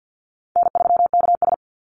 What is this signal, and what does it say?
Morse code, heard as a keyed tone, about 50 wpm, 700 Hz. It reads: N3KS
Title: Easy Call Signs - 50wpm